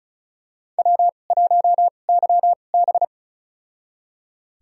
Morse code, heard as a keyed tone, about 36 wpm, spaced as for 25 wpm, 700 Hz. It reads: W1YB
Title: USA Call Signs - 4 Characters - 2x Character Spacing - 35wpm